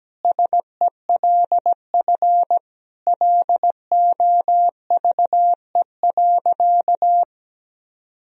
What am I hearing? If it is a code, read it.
Morse code, heard as a keyed tone, about 17 wpm, 700 Hz. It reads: SELF LOVE.
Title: Sentences from Top 1000 Words - Review Part 1 - 17wpm